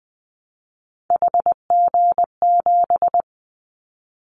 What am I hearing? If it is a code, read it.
Morse code, heard as a keyed tone, about 20 wpm, 700 Hz. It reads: HG7
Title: Callsign Prefixes for Top 50 DXCC - 20wpm